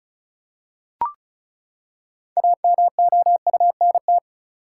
Morse code, heard as a keyed tone, about 36 wpm, 700 Hz. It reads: AMOUNT
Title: Six Letter Words - 35wpm